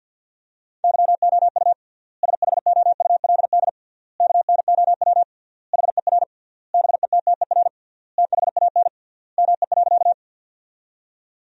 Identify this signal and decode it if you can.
Morse code, heard as a keyed tone, about 50 wpm, 700 Hz. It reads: YOU SHOULD KNOW HER BETTER THAN ME.
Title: 3x Word-Spacing - Sentences from Top 400 Words - 50wpm